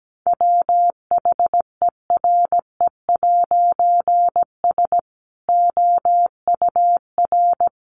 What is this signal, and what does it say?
Morse code, heard as a keyed tone, about 17 wpm, 700 Hz. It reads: WHERE'S OUR
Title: A Sound Of Thunder 17wpm